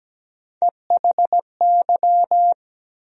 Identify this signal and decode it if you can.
Morse code, heard as a keyed tone, about 17 wpm, 700 Hz. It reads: EHY
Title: USA Call Sign Postfixes - 3 Letters 17wpm